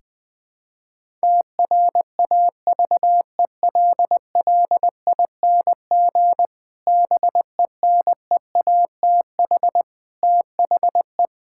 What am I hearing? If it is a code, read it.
Morse code, heard as a keyed tone, about 19 wpm, 700 Hz. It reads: TRAVELLING BENEATH THE
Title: Pangram Challenge v2 - 20wpm